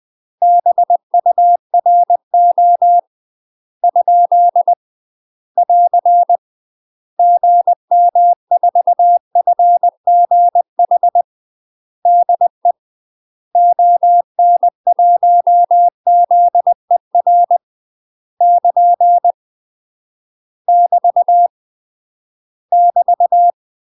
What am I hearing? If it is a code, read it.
Morse code, heard as a keyed tone, about 20 wpm, 700 Hz. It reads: BURO ? <AR> GM4FGH DE ON1ZER <KN> <BT> <BT>
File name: UK MASTER QSOs TRAINING/20WPM